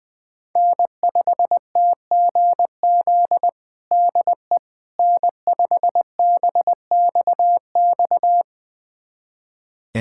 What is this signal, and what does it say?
Morse code, heard as a keyed tone, about 20 wpm, 700 Hz. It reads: N5TGZ DE N5BXX
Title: USA Callsign DE Callsign - 5 Characters - 20wpm